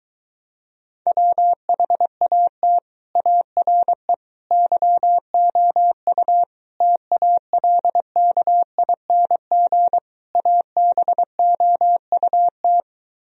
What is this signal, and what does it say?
Morse code, heard as a keyed tone, about 22 wpm, 700 Hz. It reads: WHAT ARE YOU TALKING ABOUT